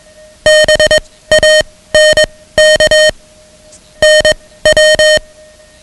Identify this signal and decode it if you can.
Morse code, heard as a keyed tone, about 21 wpm, spaced as for 16 wpm, 620 Hz. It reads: BANK NW
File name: ham radio / CodeQuick / Code Quick 10